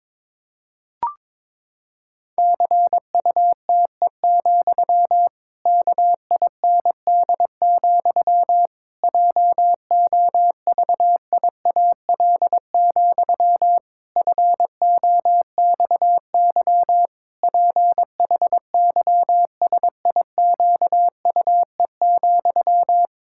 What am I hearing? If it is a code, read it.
Morse code, heard as a keyed tone, about 22 wpm, 700 Hz. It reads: CUTE, KIND, JOVIAL, FOXY PHYSIQUE,
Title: Pangram Challenge v2 - 22wpm